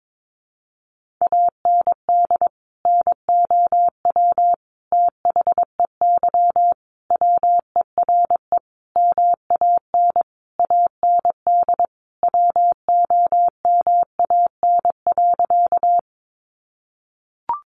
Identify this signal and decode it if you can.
Morse code, heard as a keyed tone, about 22 wpm, 700 Hz. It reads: AND NOW THEY WERE MAN AND WOMAN.
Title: Sentences from Top 1100 Words 22wpm